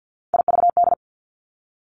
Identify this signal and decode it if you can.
Morse code, heard as a keyed tone, about 50 wpm, 700 Hz. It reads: IVD
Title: Three Challenging Letter-Number Combinations 50wpm